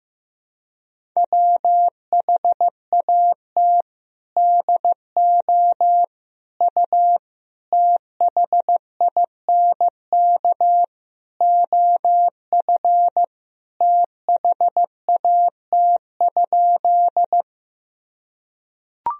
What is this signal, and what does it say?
Morse code, heard as a keyed tone, about 15 wpm, 700 Hz. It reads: WHAT DO U THINK OF THAT?
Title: Sentences from Top 200 Words and Abbreviations 15wpm